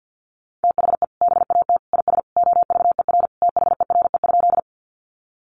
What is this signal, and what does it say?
Morse code, heard as a keyed tone, about 50 wpm, 700 Hz. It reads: THE BAT IS OVER THERE?
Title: Sentences from Top 1500 Words 50wpm